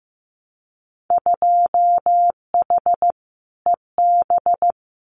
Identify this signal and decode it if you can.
Morse code, heard as a keyed tone, about 15 wpm, 700 Hz. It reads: 2H EB
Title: Winter Field Day Exchanges Encore 15wpm